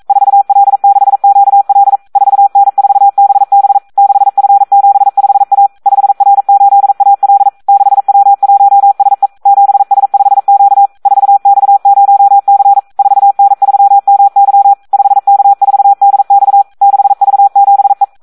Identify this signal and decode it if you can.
Morse code, heard as a keyed tone, about 42 wpm, 800 Hz. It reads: 4L6QL 4N4BB 6F75A 5R8AL 6W1SE 7S5Q 4X0C 4N3MY 5K4DX 6V7E
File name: Audio practica CW / SD 1 42